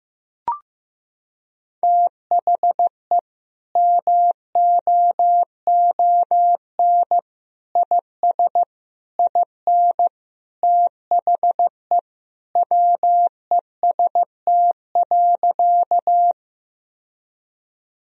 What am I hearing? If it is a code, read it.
Morse code, heard as a keyed tone, about 15 wpm, 700 Hz. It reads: THE MOON IS IN THE WEST.
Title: Sentences from Top 1000 Words - Review Part 3 - 15wpm